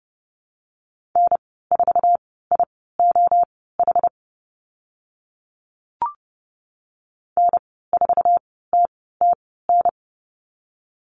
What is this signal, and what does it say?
Morse code, heard as a keyed tone, about 30 wpm, spaced as for 17 wpm, 700 Hz. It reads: N4IOH N4TTN
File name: Lesson 14g - Number 2 - Call Signs 30wpm